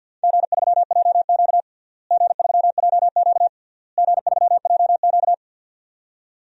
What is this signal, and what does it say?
Morse code, heard as a keyed tone, about 50 wpm, 700 Hz. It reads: G3JQ G3JQ G3JQ
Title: DXCC-100 Call Signs - 4 Characters - Repeated 3x - 50wpm